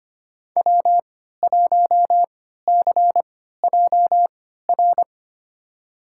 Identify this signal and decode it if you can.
Morse code, heard as a keyed tone, about 25 wpm, spaced as for 14 wpm, 700 Hz. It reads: W1CJR
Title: USA Call Signs - 5 Characters - 3x Character Spacing - 25wpm